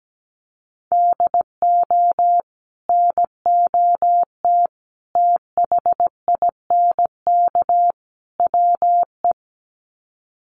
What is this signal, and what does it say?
Morse code, heard as a keyed tone, about 17 wpm, 700 Hz. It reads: DO NOT THINK WE
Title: Sets of 4 Words 17wpm